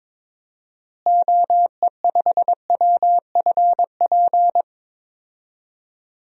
Morse code, heard as a keyed tone, about 22 wpm, 700 Hz. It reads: OE5WFP
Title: DXCC-100 Call Signs - 6 Characters - Repeated 2x - 22wpm